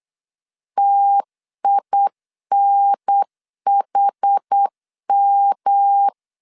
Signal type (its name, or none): Morse code